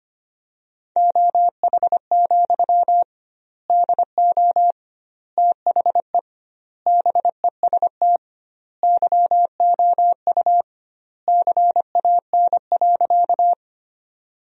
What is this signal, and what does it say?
Morse code, heard as a keyed tone, about 25 wpm, 700 Hz. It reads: OH, DO THE BEST YOU CAN.
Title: 2x Word-Spacing - Sentences from Top 500 Words - 25wpm